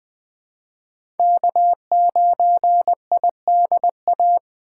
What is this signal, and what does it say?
Morse code, heard as a keyed tone, about 20 wpm, 700 Hz. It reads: K9IDA
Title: USA Call Signs - 5 Characters - 2x Character Spacing - 20wpm